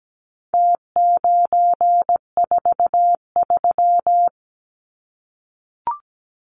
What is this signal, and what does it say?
Morse code, heard as a keyed tone, about 17 wpm, 700 Hz. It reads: T943
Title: Numbers - Contest Style - 4-Digit 17wpm